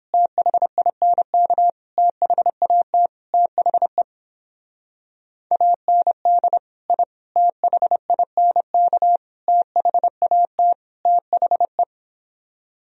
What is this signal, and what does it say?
Morse code, heard as a keyed tone, about 29 wpm, 700 Hz. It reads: THINK THAT THE AND I THINK THAT THE